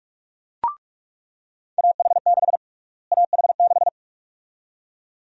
Morse code, heard as a keyed tone, about 47 wpm, 700 Hz. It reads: AH6 AH6
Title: USA Call Sign Prefixes - Repeated 2x - 45wpm